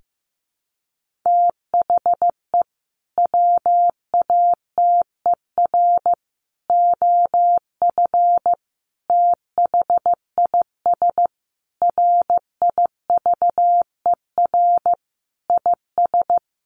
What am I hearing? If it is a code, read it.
Morse code, heard as a keyed tone, about 15 wpm, 700 Hz. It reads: THE WATER OF THIS RIVER IS
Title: Sentences from Top 1500 Words 15wpm